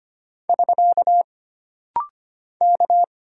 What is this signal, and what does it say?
Morse code, heard as a keyed tone, about 25 wpm, 700 Hz. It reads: <SK> K
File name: Single Character (Letter:Number:Punctuation:Prosign) - Rapid-Fire - 25wpm